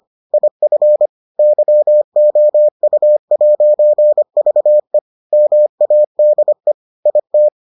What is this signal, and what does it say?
Morse code, heard as a keyed tone, about 25 wpm, 600 Hz. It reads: IF YOU'VE MADE IT